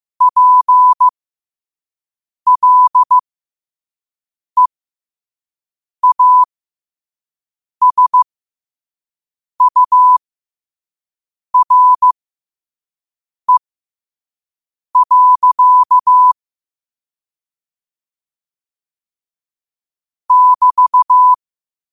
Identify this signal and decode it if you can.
Morse code, heard as a keyed tone, about 15 wpm, spaced as for 5 wpm, 1000 Hz. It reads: PLEASURE. <BT>
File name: qotdfiles / 5wpm